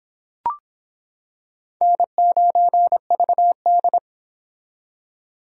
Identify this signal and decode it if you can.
Morse code, heard as a keyed tone, about 26 wpm, 700 Hz. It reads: N9VD